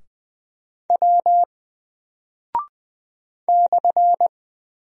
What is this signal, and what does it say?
Morse code, heard as a keyed tone, about 20 wpm, 700 Hz. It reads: W /
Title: Single Character (Letter:Number:Punctuation:Prosign) 20wpm